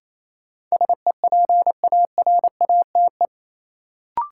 Morse code, heard as a keyed tone, about 28 wpm, 700 Hz. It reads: SEPARATE
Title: Sentences from Top 1000 Words - Review Part 7 - New Word Drill - 28wpm